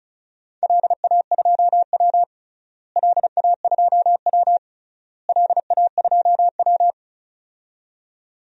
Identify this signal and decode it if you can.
Morse code, heard as a keyed tone, about 34 wpm, 700 Hz. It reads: LA2W LA2W LA2W
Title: DXCC-100 Call Signs - 4 Characters - Repeated 3x - 35wpm